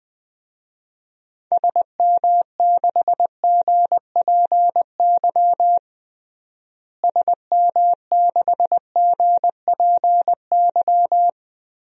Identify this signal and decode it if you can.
Morse code, heard as a keyed tone, about 20 wpm, 700 Hz. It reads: SM6GPY SM6GPY
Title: DXCC-100 Call Signs - 6 Characters - Repeated 3x - 20wpm